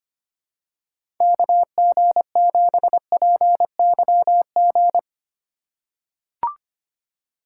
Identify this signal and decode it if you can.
Morse code, heard as a keyed tone, about 25 wpm, 700 Hz. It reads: KG7PYG
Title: USA Call Signs - 6 Characters - 2x Character Spacing - 25wpm